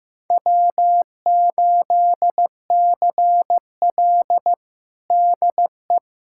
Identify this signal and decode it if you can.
Morse code, heard as a keyed tone, about 15 wpm, 700 Hz. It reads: W8CL DE
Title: USA Callsign DE Callsign - 4 Characters - 15wpm